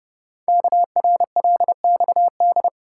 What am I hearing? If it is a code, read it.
Morse code, heard as a keyed tone, about 29 wpm, 700 Hz. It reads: KRLXD
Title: Code Groups - 5 Letters - 30wpm